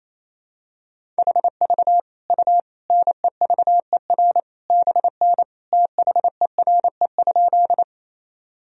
Morse code, heard as a keyed tone, about 28 wpm, 700 Hz. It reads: HV U NEVER BN THERE?
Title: Sentences from Top 300 Words and Abbreviations 28wpm